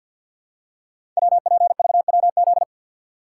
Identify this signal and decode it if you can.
Morse code, heard as a keyed tone, about 48 wpm, 690 Hz. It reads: WPVWZ
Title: Code Groups - 5 Letters - 50wpm